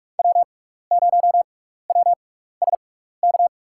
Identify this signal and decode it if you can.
Morse code, heard as a keyed tone, about 46 wpm, spaced as for 16 wpm, 700 Hz. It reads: W0WSK
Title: USA Call Signs - 6x Character Spacing - 45wpm